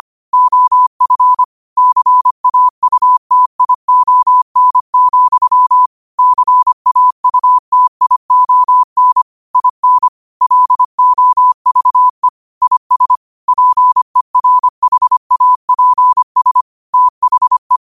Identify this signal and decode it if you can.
Morse code, heard as a keyed tone, about 25 wpm, 1000 Hz. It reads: OF CAUTION, CAUTION IN LOVE IS PERHAPS THE
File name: qotdfiles / 25wpm